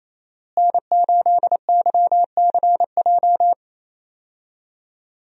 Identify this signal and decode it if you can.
Morse code, heard as a keyed tone, about 28 wpm, 700 Hz. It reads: N8YCJ
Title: USA Call Signs - 5 Characters - Repeated 3x - 28wpm